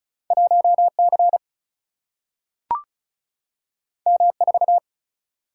Morse code, heard as a keyed tone, about 36 wpm, 700 Hz. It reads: 1C M4
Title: Single-Letter Single-Number 35wpm